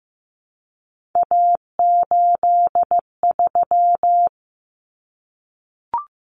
Morse code, heard as a keyed tone, about 15 wpm, 700 Hz. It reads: A83
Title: Common Cut Numbers - 3-Digit 15wpm